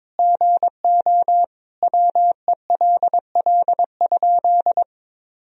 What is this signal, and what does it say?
Morse code, heard as a keyed tone, about 22 wpm, 700 Hz. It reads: GO WELL?
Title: Sentences from Top 1000 Words - Review Part 4 - 22wpm